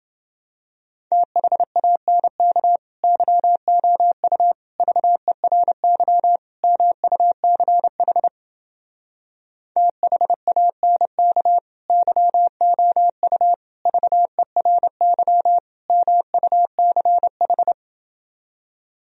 Morse code, heard as a keyed tone, about 29 wpm, 700 Hz. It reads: THANK YOU VERY MUCH THANK YOU VERY MUCH